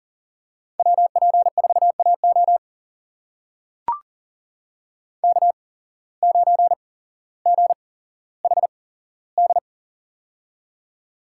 Morse code, heard as a keyed tone, about 40 wpm, spaced as for 17 wpm, 690 Hz. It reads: WP4AO K9GHD
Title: USA Call Signs - 8x Character Spacing - 40wpm